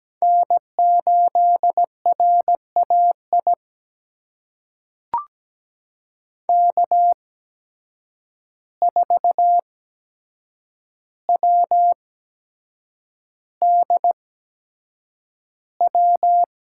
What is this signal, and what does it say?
Morse code, heard as a keyed tone, about 17 wpm, spaced as for 7 wpm, 700 Hz. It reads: N8RAI K4WDW
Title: USA Call Signs - 5 Characters - 8x Character Spacing - 17wpm